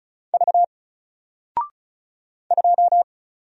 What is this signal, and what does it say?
Morse code, heard as a keyed tone, about 36 wpm, 700 Hz. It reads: V 2
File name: Single-Letter-Number-35wpm